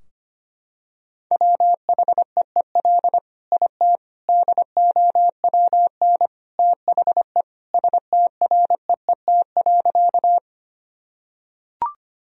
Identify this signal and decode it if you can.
Morse code, heard as a keyed tone, about 25 wpm, 700 Hz. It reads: WHEEL IT DOWN THE STREET.
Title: Sentences from Top 1000 Words - Review Part 3 - 25wpm